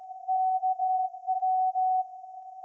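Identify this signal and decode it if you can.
Morse code, heard as a keyed tone, about 16 wpm, 750 Hz. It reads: KW
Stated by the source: Morse code from SAQ, received at a station in Putignano, Apulia, Italy, on VLF.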